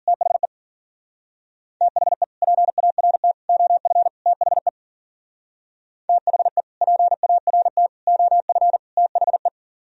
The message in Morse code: THE THE PART OF THE THE PART OF THE

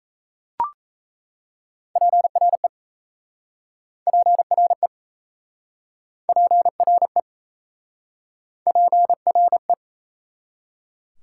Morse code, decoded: PRE PRE PRE PRE

